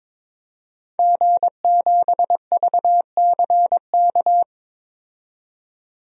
Morse code, heard as a keyed tone, about 22 wpm, 700 Hz. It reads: G7VCK
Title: DXCC-100 Call Signs - 5 Characters - 22wpm